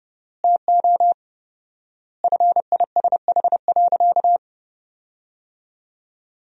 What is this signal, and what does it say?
Morse code, heard as a keyed tone, about 30 wpm, 700 Hz. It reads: TO FISH.
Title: 4x Word-Spacing - Sentences from Top 300 Words - 30wpm